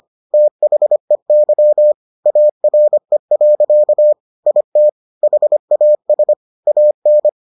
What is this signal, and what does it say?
Morse code, heard as a keyed tone, about 25 wpm, 600 Hz. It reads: THEY ARE. IT HAS AN